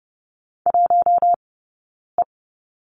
Morse code, heard as a keyed tone, about 30 wpm, spaced as for 9 wpm, 690 Hz. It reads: 1E
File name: Lesson 18d - Letter m - Two Characters 30wpm